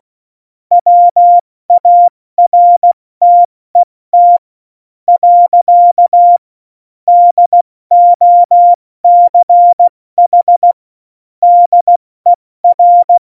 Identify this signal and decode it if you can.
Morse code, heard as a keyed tone, about 16 wpm, 700 Hz. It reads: WARTET . DOCH DER
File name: Morsekurs / Witze / Witz 13 March 2018, 70 F80